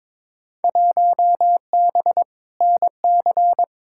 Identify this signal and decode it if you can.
Morse code, heard as a keyed tone, about 22 wpm, 700 Hz. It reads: 1B NC
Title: Field Day Exchanges Encore - 22wpm